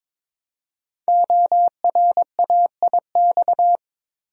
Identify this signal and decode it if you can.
Morse code, heard as a keyed tone, about 22 wpm, 700 Hz. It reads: ORAIX
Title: Code Groups - 5 Letters - 22wpm